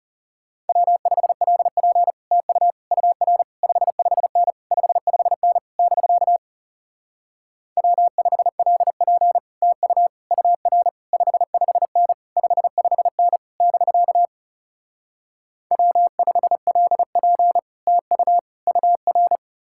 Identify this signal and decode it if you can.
Morse code, heard as a keyed tone, about 40 wpm, 700 Hz. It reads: W5LP TU UR 55N 55N <BK> W5LP TU UR 55N 55N <BK> W5LP TU UR